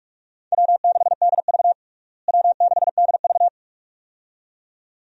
Morse code, heard as a keyed tone, about 46 wpm, 700 Hz. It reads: W6DV W6DV
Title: USA Call Signs - 4 Characters - Repeated 2x - 45wpm